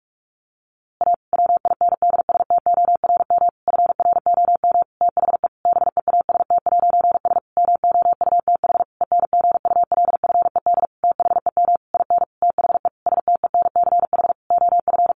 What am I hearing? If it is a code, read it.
Morse code, heard as a keyed tone, about 44 wpm, 700 Hz. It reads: A WINDSTORM FROM THE BEAST'S MOUTH ENGULFED THEM IN THE STENCH OF
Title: A Sound Of Thunder 45wpm